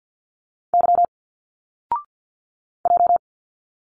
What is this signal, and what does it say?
Morse code, heard as a keyed tone, about 48 wpm, 700 Hz. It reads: Y J
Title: Single Letters 50wpm